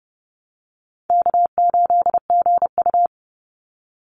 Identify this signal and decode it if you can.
Morse code, heard as a keyed tone, about 30 wpm, 700 Hz. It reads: K8GU
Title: Easy Call Signs Repeated 3x - 30wpm